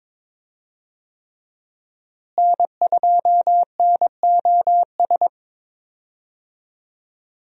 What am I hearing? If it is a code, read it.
Morse code, heard as a keyed tone, about 22 wpm, 700 Hz. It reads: N2NOS
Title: USA Call Signs - 5 Characters - Repeated 3x - 22wpm